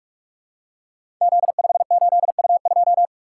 Morse code, heard as a keyed tone, about 44 wpm, 700 Hz. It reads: Z58U2
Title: Code Groups - 5 Letter:Numbers - 45wpm